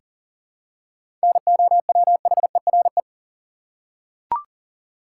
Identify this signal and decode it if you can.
Morse code, heard as a keyed tone, about 40 wpm, 700 Hz. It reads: NOWHERE
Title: Seven Letter Words - 40wpm